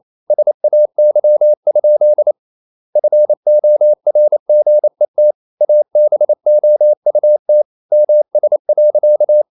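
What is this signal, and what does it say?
Morse code, heard as a keyed tone, about 28 wpm, 600 Hz. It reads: SAY? FORGET ABOUT MS.
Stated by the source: Morse code practice at 28 wpm, text of Ken Follett's novel Never in English, from CW med Gnister.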